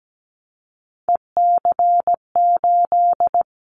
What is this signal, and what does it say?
Morse code, heard as a keyed tone, about 17 wpm, 700 Hz. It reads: EC8
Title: Callsign Prefixes for Top 100 DXCC - 17wpm